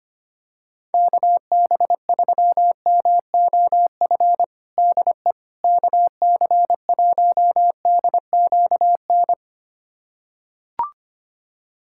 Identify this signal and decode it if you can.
Morse code, heard as a keyed tone, about 25 wpm, 700 Hz. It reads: KB3MOF DE KC1DQN
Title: USA Callsign DE Callsign 25wpm